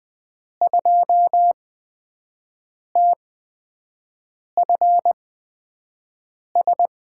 Morse code, heard as a keyed tone, about 20 wpm, spaced as for 5 wpm, 700 Hz. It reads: 2TFS
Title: USA Call Signs - 8x Character Spacing - 20wpm